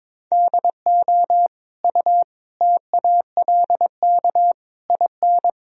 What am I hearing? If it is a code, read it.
Morse code, heard as a keyed tone, about 22 wpm, 700 Hz. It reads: DO U TALK IN
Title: Sentences from Top 500 Words and Abbreviations 22wpm